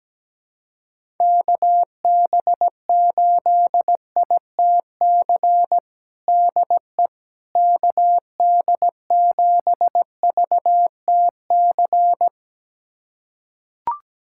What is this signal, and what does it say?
Morse code, heard as a keyed tone, about 17 wpm, 700 Hz. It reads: KB8ITC DE KD7VTC
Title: USA Callsign DE Callsign - 6 Characters - 17wpm